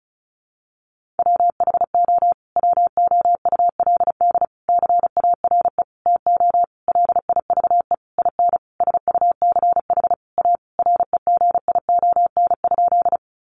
Morse code, heard as a keyed tone, about 34 wpm, 700 Hz. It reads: WHO WOULD CARE TO LIVE IN SUCH A REGION?
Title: Sentences from Top 1500 Words 35wpm